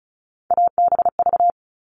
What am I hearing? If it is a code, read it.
Morse code, heard as a keyed tone, about 34 wpm, 700 Hz. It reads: ABV
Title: All Cut Numbers - 3-Digit 35wpm